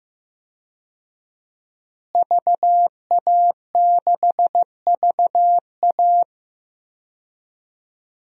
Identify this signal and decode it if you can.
Morse code, heard as a keyed tone, about 16 wpm, 700 Hz. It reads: VA6VA
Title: DXCC-100 Call Signs - 5 Characters - Repeated 3x - 15wpm